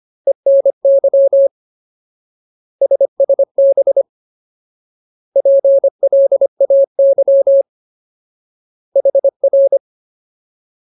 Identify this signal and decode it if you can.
Morse code, heard as a keyed tone, about 25 wpm, 550 Hz. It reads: ENY SSB PLAY HR
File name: Ultimate Word List / 25WPM Ultimate